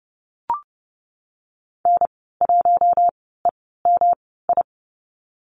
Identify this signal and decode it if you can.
Morse code, heard as a keyed tone, about 30 wpm, 700 Hz. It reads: N 1 E M I
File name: Lesson 19g - Letter w - Call Signs 30wpm